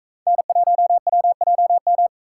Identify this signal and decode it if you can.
Morse code, heard as a keyed tone, about 42 wpm, 700 Hz. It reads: N1WJM